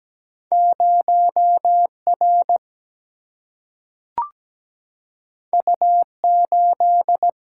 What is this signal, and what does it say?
Morse code, heard as a keyed tone, about 17 wpm, 700 Hz. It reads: 0R U8
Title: Single-Letter Single-Number 17wpm